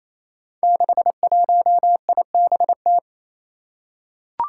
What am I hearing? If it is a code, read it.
Morse code, heard as a keyed tone, about 27 wpm, 700 Hz. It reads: 61IBT
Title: Five Challenging Letter-Number Combinations 28wpm